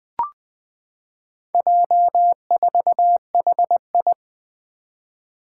Morse code, heard as a keyed tone, about 20 wpm, 700 Hz. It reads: J4HI
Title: Four Challenging Letter-Number Combinations 20wpm